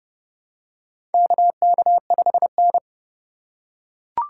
Morse code, heard as a keyed tone, about 30 wpm, 700 Hz. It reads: KK5N
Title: USA Call Signs - 3x Character Spacing - 30wpm